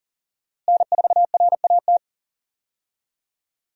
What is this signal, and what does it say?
Morse code, heard as a keyed tone, about 40 wpm, 700 Hz. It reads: N4RAT
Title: USA Call Signs - 5 Characters - Repeated 2x - 40wpm